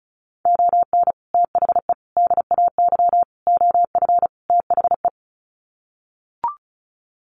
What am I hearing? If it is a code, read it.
Morse code, heard as a keyed tone, about 34 wpm, 700 Hz. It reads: ON THE DAY OF THE
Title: Sets of 5 Words Encore 35wpm